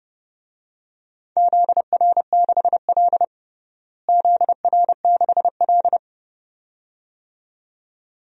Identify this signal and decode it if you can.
Morse code, heard as a keyed tone, about 29 wpm, 700 Hz. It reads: ZR6L ZR6L
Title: DXCC-100 Call Signs - 4 Characters - Repeated 2x - 30wpm